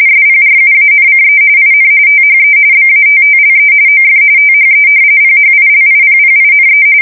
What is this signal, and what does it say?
RTTY, about 45.45 baud. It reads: SPRUANCE IN THE BATTLE OF THE PHILIPPINE
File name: fsk170 / battleshipnj